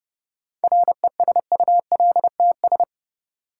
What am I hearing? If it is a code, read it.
Morse code, heard as a keyed tone, about 30 wpm, 700 Hz. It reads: RESULTS
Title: Sets of 4 Words - New Word Drill - 30wpm